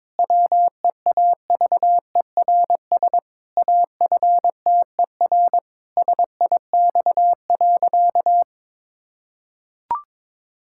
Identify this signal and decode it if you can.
Morse code, heard as a keyed tone, about 22 wpm, 700 Hz. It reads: WEAVERS AFTER SIX.
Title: Pangram Challenge v2 - 22wpm